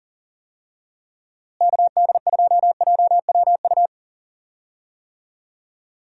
Morse code, heard as a keyed tone, about 40 wpm, 700 Hz. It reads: KD2JWU
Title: USA Call Signs - 6 Characters - Repeated 3x - 40wpm